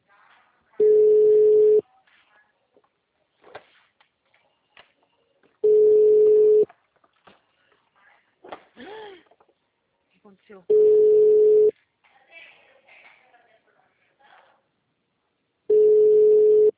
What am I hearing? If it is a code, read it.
Ringback tone (Europe, 425 Hz), with speech.